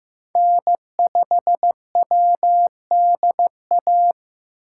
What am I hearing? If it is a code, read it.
Morse code, heard as a keyed tone, about 15 wpm, 700 Hz. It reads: N5WDA
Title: USA Call Signs - 4x Character Spacing - 15wpm